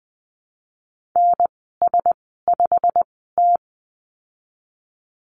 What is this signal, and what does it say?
Morse code, heard as a keyed tone, about 20 wpm, spaced as for 15 wpm, 700 Hz. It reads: NS5T
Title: 147 Lesson 24g - Letter y - Call Signs 20wpm